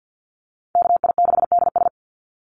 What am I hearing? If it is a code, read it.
Morse code, heard as a keyed tone, about 49 wpm, 700 Hz. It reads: KI6DS
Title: NA SOTA Chaser Callsigns 50wpm